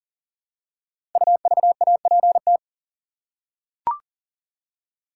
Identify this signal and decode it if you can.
Morse code, heard as a keyed tone, about 40 wpm, 700 Hz. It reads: UVAPT
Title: Code Groups - 5 Letter:Numbers - 40wpm